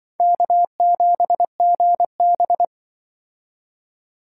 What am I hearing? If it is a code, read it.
Morse code, heard as a keyed tone, about 24 wpm, 700 Hz. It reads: K7GB
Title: Speed Racing - USA 1x2 Call Signs 20wpm